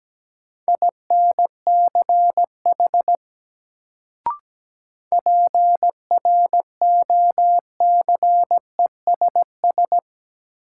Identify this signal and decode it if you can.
Morse code, heard as a keyed tone, about 17 wpm, 700 Hz. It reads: INCH PROCESS
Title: Sentences from Top 1000 Words - Review Part 3 - New Word Drill - 17wpm